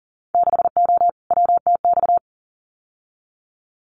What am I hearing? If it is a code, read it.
Morse code, heard as a keyed tone, about 41 wpm, 700 Hz. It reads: 6O WTX
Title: Winter Field Day Exchanges Encore 40wpm